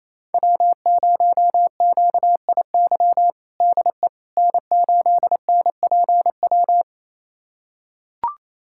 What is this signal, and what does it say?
Morse code, heard as a keyed tone, about 28 wpm, 700 Hz. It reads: W0QIY DE N8NPW
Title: USA Callsign DE Callsign - 5 Characters - 28wpm